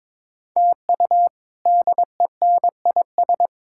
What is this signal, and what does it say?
Morse code, heard as a keyed tone, about 22 wpm, 700 Hz. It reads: TU DENIS
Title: K1USN SST QSO Practice - 22wpm